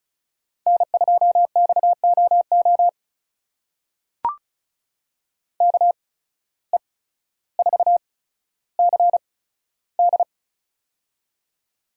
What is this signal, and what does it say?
Morse code, heard as a keyed tone, about 36 wpm, spaced as for 15 wpm, 690 Hz. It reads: N2XOO KE4CD
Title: USA Call Signs - 5 Characters - 8x Character Spacing - 35wpm